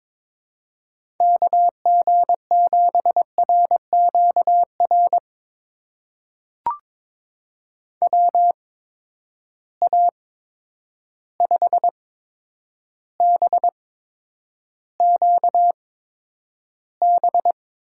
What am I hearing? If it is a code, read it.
Morse code, heard as a keyed tone, about 22 wpm, spaced as for 9 wpm, 700 Hz. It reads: KG7RQR WA5BQB
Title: USA Call Signs - 6 Characters - 8x Character Spacing - 22wpm